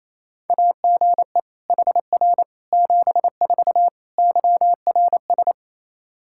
Morse code, heard as a keyed tone, about 28 wpm, 700 Hz. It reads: AGE HR 74 YRS
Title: Practice QSO - Exchange 3 - More Info - 28wpm